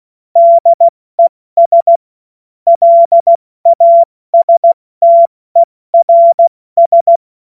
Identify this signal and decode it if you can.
Morse code, heard as a keyed tone, about 16 wpm, 680 Hz. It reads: DES LASTERS
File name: Morsekurs / Witze / Witz 27. Januar 2026 70 F80